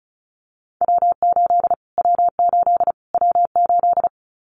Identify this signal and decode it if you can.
Morse code, heard as a keyed tone, about 34 wpm, 700 Hz. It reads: W8 W8 W8
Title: Callsign Prefixes for Top 25 DXCC - Repeated 3x - 35wpm